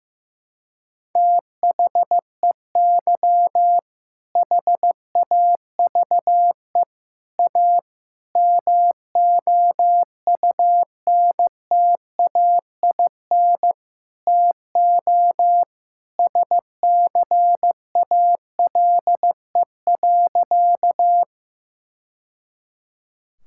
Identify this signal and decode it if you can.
Morse code, heard as a keyed tone, about 15 wpm, 700 Hz. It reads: THEY HAVE A MOUNTAIN TO SCALE.
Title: Sentences from Top 1000 Words - Review Part 5 - 15wpm